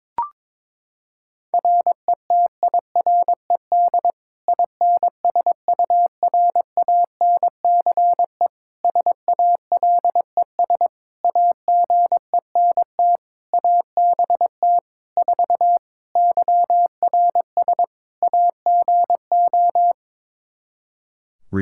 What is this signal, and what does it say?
Morse code, heard as a keyed tone, about 22 wpm, 700 Hz. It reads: RETIRED INSURANCE SALES AGENT ABT 4 YRS AGO
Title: K1USN SST QSO Practice 22wpm